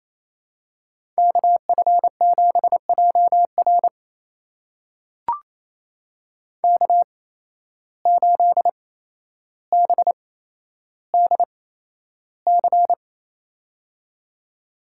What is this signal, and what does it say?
Morse code, heard as a keyed tone, about 28 wpm, spaced as for 12 wpm, 700 Hz. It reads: KF7JR K8BDC
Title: USA Call Signs - 5 Characters - 8x Character Spacing - 28wpm